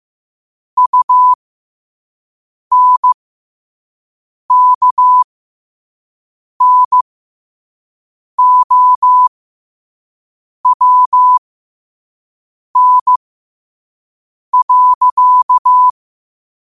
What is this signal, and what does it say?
Morse code, heard as a keyed tone, about 15 wpm, spaced as for 5 wpm, 1000 Hz. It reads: UNKNOWN.
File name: qotdfiles / 5wpm